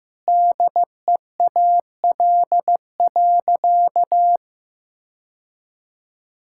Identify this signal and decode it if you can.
Morse code, heard as a keyed tone, about 15 wpm, 700 Hz. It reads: DEAL.
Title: Sentences from Top 1000 Words - Review Part 6 - 15wpm